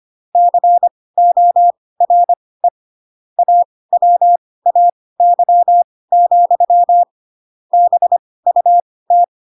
Morse code, heard as a keyed tone, about 25 wpm, spaced as for 18 wpm, 700 Hz. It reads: CORE AWAY, BUT
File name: Pinocchio / Pinocchio ch7 12 25 ik1hge